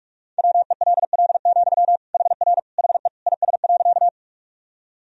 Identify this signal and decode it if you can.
Morse code, heard as a keyed tone, about 46 wpm, 700 Hz. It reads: WELL, HR HE IS.
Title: Sentences from Top 200 Words and Abbreviations 45wpm